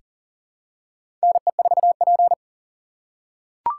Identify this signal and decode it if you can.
Morse code, heard as a keyed tone, about 40 wpm, 700 Hz. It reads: NE4P
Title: USA Call Signs - 4 Characters - 4x Character Spacing - 40wpm